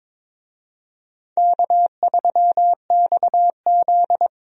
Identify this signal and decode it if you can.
Morse code, heard as a keyed tone, about 21 wpm, 700 Hz. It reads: K3XZ
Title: USA Call Signs - 4 Characters - 22wpm